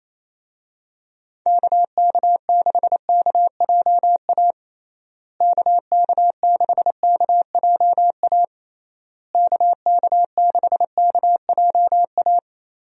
Morse code, heard as a keyed tone, about 28 wpm, 700 Hz. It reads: KK6KJA KK6KJA KK6KJA
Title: USA Call Signs - 6 Characters - Repeated 3x - 28wpm